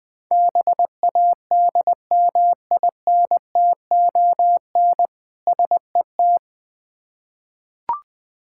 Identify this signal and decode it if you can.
Morse code, heard as a keyed tone, about 20 wpm, 690 Hz. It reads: BADMINTON SET
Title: Difficult Words with Hints - 20wpm